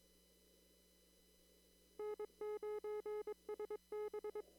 Morse code, heard as a keyed tone, about 22 wpm, 410 Hz. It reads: N9SB
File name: CW / kt5j wpx cw-op-k5pi